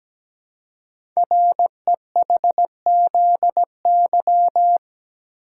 Morse code, heard as a keyed tone, about 17 wpm, 700 Hz. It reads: REHZY
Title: Code Groups - 5 Letters - 17wpm